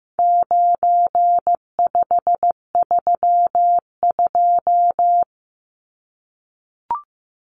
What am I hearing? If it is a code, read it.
Morse code, heard as a keyed tone, about 15 wpm, 700 Hz. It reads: 9532
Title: Numbers - 4-Digit 15wpm